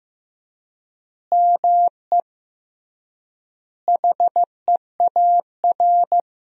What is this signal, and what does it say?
Morse code, heard as a keyed tone, about 15 wpm, 700 Hz. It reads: ME HEAR
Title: 3x Word-Spacing - Sentences from Top 300 Words - 15wpm